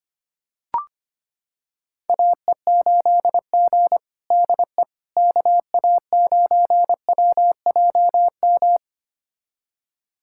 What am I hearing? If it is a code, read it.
Morse code, heard as a keyed tone, about 25 wpm, 700 Hz. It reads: AE8G DE KA9WJM
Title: USA Callsign DE Callsign 25wpm